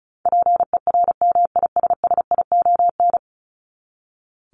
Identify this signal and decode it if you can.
Morse code, heard as a keyed tone, about 34 wpm, 700 Hz. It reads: PERMISSION
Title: Top 1501-1750 Words 35wpm